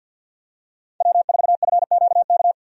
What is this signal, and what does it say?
Morse code, heard as a keyed tone, about 48 wpm, 700 Hz. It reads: W4FQK
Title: USA Call Signs - 5 Characters - 50wpm